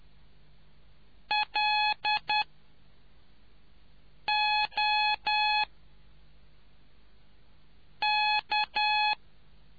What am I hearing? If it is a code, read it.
Morse code, heard as a keyed tone, about 10 wpm, 850 Hz. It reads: L O K